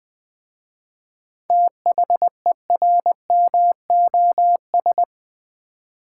Morse code, heard as a keyed tone, about 21 wpm, 700 Hz. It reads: THERMOS
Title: Easy Words with Hints - 20wpm